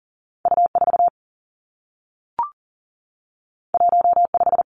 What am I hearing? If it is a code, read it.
Morse code, heard as a keyed tone, about 38 wpm, 700 Hz. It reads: U 4 1 5
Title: Two Challenging Letter-Number Combinations 40wpm